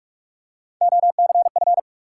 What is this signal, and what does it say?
Morse code, heard as a keyed tone, about 42 wpm, 700 Hz. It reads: OCF
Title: Top 180 QSO Elements 45wpm